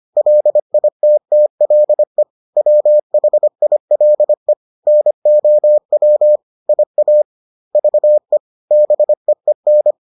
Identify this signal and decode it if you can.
Morse code, heard as a keyed tone, about 25 wpm, 600 Hz. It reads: LITTLE WHILE NOW IA VE BEEN